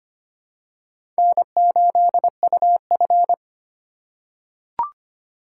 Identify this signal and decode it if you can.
Morse code, heard as a keyed tone, about 25 wpm, 700 Hz. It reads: N8UF
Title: USA Call Signs - 4 Characters - 6x Character Spacing - 25wpm